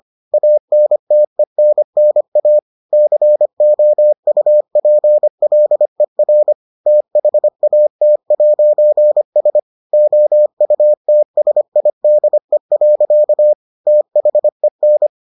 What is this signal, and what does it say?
Morse code, heard as a keyed tone, about 25 wpm, 600 Hz. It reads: ANTENNA COUPLER THAT'S OUTSIDE. THEN